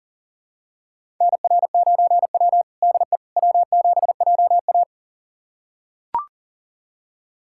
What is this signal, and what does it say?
Morse code, heard as a keyed tone, about 40 wpm, 700 Hz. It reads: NR9W DE W7JA
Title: USA Callsign DE Callsign - 4 Characters - 40wpm